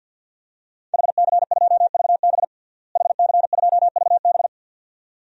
Morse code, heard as a keyed tone, about 47 wpm, 700 Hz. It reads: HC2VB HC2VB
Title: DXCC-100 Call Signs - 5 Characters - Repeated 2x - 50wpm